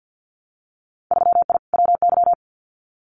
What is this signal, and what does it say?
Morse code, heard as a keyed tone, about 50 wpm, 700 Hz. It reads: 3I WY
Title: Winter Field Day Exchanges Encore 50wpm